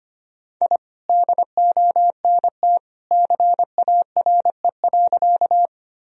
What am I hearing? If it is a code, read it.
Morse code, heard as a keyed tone, about 25 wpm, 700 Hz. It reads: I DONT CARE.
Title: Sentences from Top 1000 Words - Review Part 1 - 25wpm